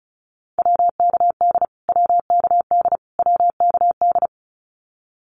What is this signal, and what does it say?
Morse code, heard as a keyed tone, about 34 wpm, 700 Hz. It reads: WKD WKD WKD
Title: Abbreviations Repeated 3x - 35wpm